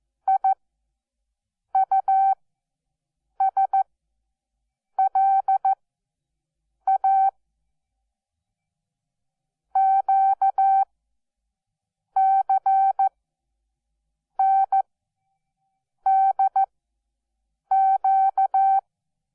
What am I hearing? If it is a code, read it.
Morse code, heard as a keyed tone, about 14 wpm, spaced as for 6 wpm, 770 Hz. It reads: IUSLA QCNDQ
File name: Onlinelehrgang / morsen / DJ4UF Morsen12